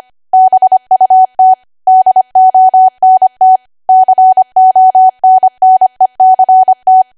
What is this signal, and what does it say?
Morse code, heard as a keyed tone, about 25 wpm, 750 Hz. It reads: BUT DONT CONNECT
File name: Archive / 25 WPM / 25WPM